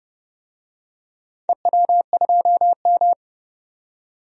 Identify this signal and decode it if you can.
Morse code, heard as a keyed tone, about 31 wpm, 700 Hz. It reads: EW2M
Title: DXCC-100 Call Signs - 4 Characters - 30wpm